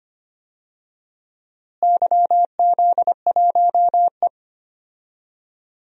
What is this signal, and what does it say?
Morse code, heard as a keyed tone, about 25 wpm, 700 Hz. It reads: YZ1E
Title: DXCC-100 Call Signs - 4 Characters - Repeated 3x - 25wpm